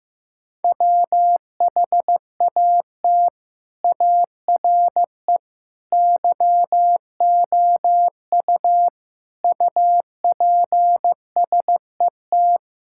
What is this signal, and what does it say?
Morse code, heard as a keyed tone, about 15 wpm, 700 Hz. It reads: WHAT ARE YOU UPSET
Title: Sentences from Top 1000 Words - Review Part 5 - 15wpm